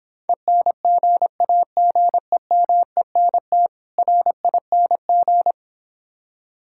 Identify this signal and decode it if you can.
Morse code, heard as a keyed tone, about 26 wpm, 700 Hz. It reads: ENGAGEMENT RING